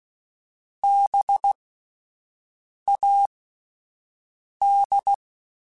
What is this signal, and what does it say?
Morse code, heard as a keyed tone, about 16 wpm, spaced as for 5 wpm, 800 Hz. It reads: BAD
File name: Spaced Alphas / Spaced Alphas mp3s / Bs sp @10